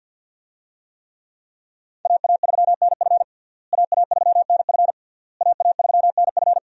Morse code, heard as a keyed tone, about 51 wpm, 700 Hz. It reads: AA3NF AA3NF AA3NF
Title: USA Call Signs - 5 Characters - Repeated 3x - 50wpm